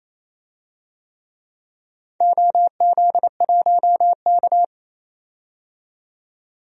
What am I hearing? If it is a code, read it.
Morse code, heard as a keyed tone, about 28 wpm, 700 Hz. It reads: OZ1K
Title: DXCC-100 Call Signs - 4 Characters - 28wpm